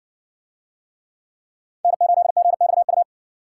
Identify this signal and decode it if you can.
Morse code, heard as a keyed tone, about 57 wpm, 700 Hz. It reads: N8GXU